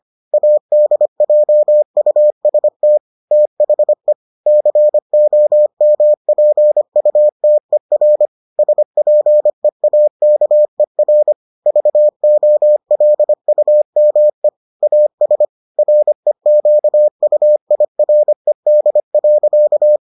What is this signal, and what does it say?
Morse code, heard as a keyed tone, about 25 wpm, 600 Hz. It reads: ADJUST THE COMPUTER SPEAKER VOLUME AS REQUIRED.